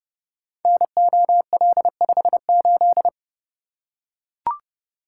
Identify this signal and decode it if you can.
Morse code, heard as a keyed tone, about 30 wpm, 700 Hz. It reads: NOL58
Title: Code Groups - 5 Letter:Numbers - 30wpm